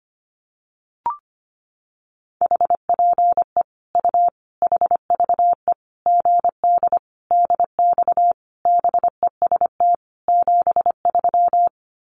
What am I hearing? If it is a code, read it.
Morse code, heard as a keyed tone, about 25 wpm, 700 Hz. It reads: HPE U HVE GD DX BEST 73
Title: Common QSO Phrases 25wpm